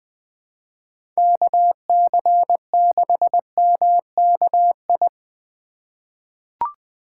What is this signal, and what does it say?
Morse code, heard as a keyed tone, about 21 wpm, 700 Hz. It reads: KC6MKI
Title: USA Call Signs - 6 Characters - 3x Character Spacing - 20wpm